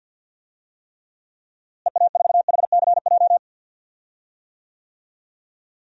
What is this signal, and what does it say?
Morse code, heard as a keyed tone, about 52 wpm, 700 Hz. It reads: EA4HCJ
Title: DXCC-100 Call Signs - 6 Characters - Repeated 2x - 50wpm